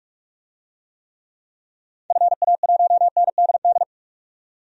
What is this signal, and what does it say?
Morse code, heard as a keyed tone, about 46 wpm, 700 Hz. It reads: FA1NDD
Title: DXCC-100 Call Signs - 6 Characters - 45wpm